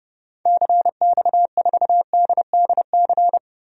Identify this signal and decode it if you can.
Morse code, heard as a keyed tone, about 30 wpm, 700 Hz. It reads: CX4DDC
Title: DXCC-100 Call Signs - 6 Characters - Repeated 3x - 30wpm